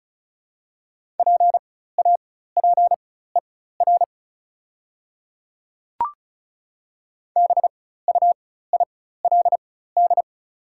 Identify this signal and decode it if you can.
Morse code, heard as a keyed tone, about 36 wpm, spaced as for 16 wpm, 690 Hz. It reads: PAPER BUILD
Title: 4x Character Spacing - Top 500 Words - 35wpm